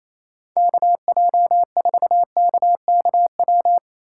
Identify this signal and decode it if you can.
Morse code, heard as a keyed tone, about 28 wpm, 700 Hz. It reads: KJ4KKW
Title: USA Call Signs - 6 Characters - 2x Character Spacing - 28wpm